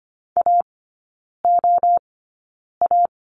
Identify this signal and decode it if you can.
Morse code, heard as a keyed tone, about 25 wpm, spaced as for 9 wpm, 690 Hz. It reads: AOA
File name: Lesson 14e - Number 2 - Three Characters 25wpm